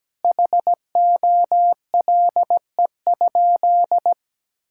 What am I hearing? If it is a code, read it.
Morse code, heard as a keyed tone, about 17 wpm, 700 Hz. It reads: HOLE?
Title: Sentences from Top 1600 Words 17wpm